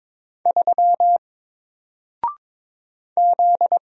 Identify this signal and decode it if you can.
Morse code, heard as a keyed tone, about 22 wpm, 700 Hz. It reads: 3 Z
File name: Single-Letter-Number-22wpm